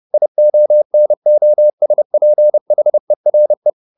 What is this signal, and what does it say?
Morse code, heard as a keyed tone, about 30 wpm, 600 Hz. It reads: IONOSPHERE